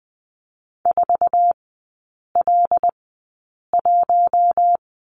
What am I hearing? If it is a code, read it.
Morse code, heard as a keyed tone, about 20 wpm, spaced as for 8 wpm, 700 Hz. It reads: 4L1
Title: 114 Lesson 19e - Letter w - Three Characters 20wpm